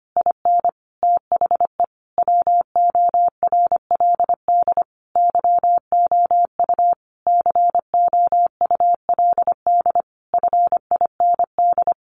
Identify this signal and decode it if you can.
Morse code, heard as a keyed tone, about 25 wpm, 700 Hz. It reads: IN THE WORLD YOU COULD FIND
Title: Sentences from Top 1400 Words 25wpm